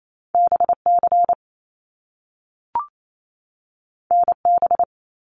Morse code, heard as a keyed tone, about 28 wpm, 700 Hz. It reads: BC NB
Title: Canadian Provinces and Territories 28wpm